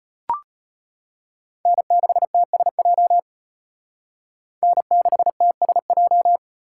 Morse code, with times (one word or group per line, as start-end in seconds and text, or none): 1.7-3.2 N6TSJ
4.6-6.4 N6TSJ